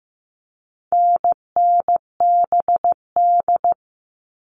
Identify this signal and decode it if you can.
Morse code, heard as a keyed tone, about 15 wpm, 700 Hz. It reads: NNBD
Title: All Cut Numbers - 4-Digit 15wpm